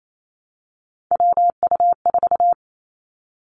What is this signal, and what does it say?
Morse code, heard as a keyed tone, about 28 wpm, 700 Hz. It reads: WU4
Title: Three Challenging Letter-Number Combinations 28wpm